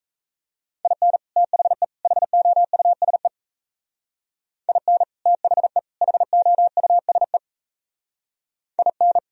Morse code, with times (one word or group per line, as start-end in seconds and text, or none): 0.8-1.2 IN
1.4-1.8 THE
2.0-3.3 HOUSE
4.7-5.0 IN
5.3-5.8 THE
6.0-7.4 HOUSE
8.8-9.2 IN